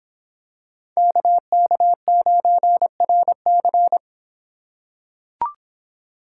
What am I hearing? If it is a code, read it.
Morse code, heard as a keyed tone, about 26 wpm, 700 Hz. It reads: KK9RC